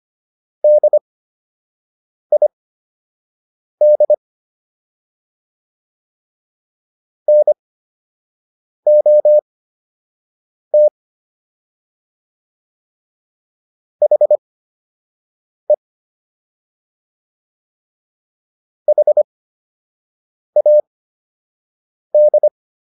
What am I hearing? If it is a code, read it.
Morse code, heard as a keyed tone, about 25 wpm, spaced as for 6 wpm, 600 Hz. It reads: DID NOT HE HAD